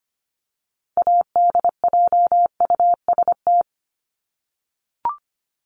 Morse code, heard as a keyed tone, about 25 wpm, 700 Hz. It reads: ADJUST
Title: Top 1501-1750 Words 25wpm